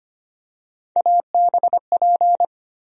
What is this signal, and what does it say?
Morse code, heard as a keyed tone, about 25 wpm, 700 Hz. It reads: ABP
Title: USA Call Sign Postfixes - 3 Letters 25wpm